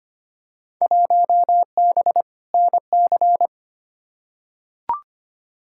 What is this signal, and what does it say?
Morse code, heard as a keyed tone, about 25 wpm, 700 Hz. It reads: 1B NC
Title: Field Day Exchanges - 25wpm